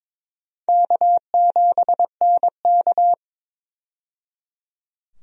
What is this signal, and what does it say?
Morse code, heard as a keyed tone, about 22 wpm, 700 Hz. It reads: K7NK
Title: USA Call Signs - 4 Characters - 22wpm